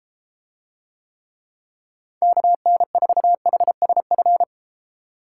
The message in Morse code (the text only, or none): KN4HSF